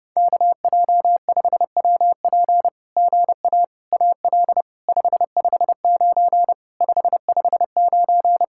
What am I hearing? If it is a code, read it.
Morse code, heard as a keyed tone, about 28 wpm, 700 Hz. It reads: KJ5WP GA AL 559 559
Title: SOTA QSO Practice 30wpm